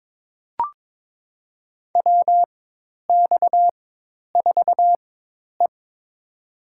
Morse code, heard as a keyed tone, about 22 wpm, spaced as for 10 wpm, 700 Hz. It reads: WX4E
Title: USA Call Signs - 4 Characters - 4x Character Spacing - 22wpm